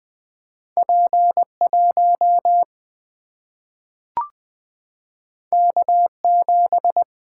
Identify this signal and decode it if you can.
Morse code, heard as a keyed tone, about 20 wpm, 700 Hz. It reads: P1 K7
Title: Single-Letter Single-Number 20wpm